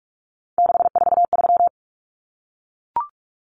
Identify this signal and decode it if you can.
Morse code, heard as a keyed tone, about 44 wpm, 700 Hz. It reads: 643
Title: Common Cut Numbers - 3-Digit 45wpm